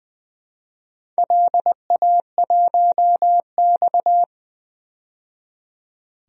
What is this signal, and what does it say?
Morse code, heard as a keyed tone, about 20 wpm, 700 Hz. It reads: LA1X
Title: DXCC-100 Call Signs - 4 Characters - 20wpm